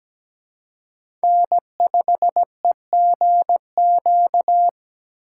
Morse code, heard as a keyed tone, about 20 wpm, 700 Hz. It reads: N5EGQ
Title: Speed Racing - USA 1x3 Call Signs 17wpm